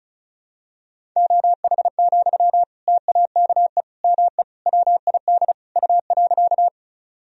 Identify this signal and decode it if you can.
Morse code, heard as a keyed tone, about 36 wpm, 700 Hz. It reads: OH, TAKE ME WID U.
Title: Sentences from Top 500 Words and Abbreviations 35wpm